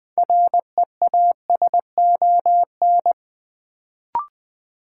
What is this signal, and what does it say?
Morse code, heard as a keyed tone, about 20 wpm, 700 Hz. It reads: REASON
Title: Sentences from Top 700 Words - New Word Drill - 20wpm